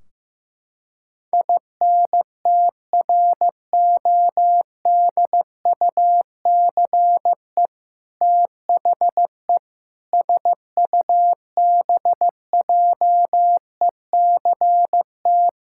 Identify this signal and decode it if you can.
Morse code, heard as a keyed tone, about 15 wpm, 700 Hz. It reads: INTRODUCE THE SUBJECT
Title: Sentences from Top 1600 Words 15wpm